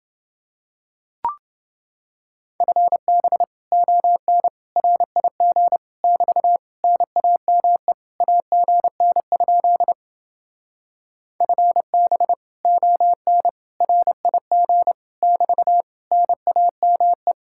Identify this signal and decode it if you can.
Morse code, heard as a keyed tone, about 29 wpm, 700 Hz. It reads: FB ON RIG <BT> NAME AGN? FB ON RIG <BT> NAME